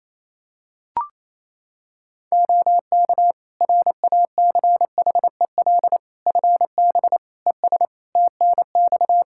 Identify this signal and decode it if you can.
Morse code, heard as a keyed tone, about 27 wpm, 700 Hz. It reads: OK RACHEL FB ES TNX
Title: Practice QSO - Exchange 2 - Info - 28wpm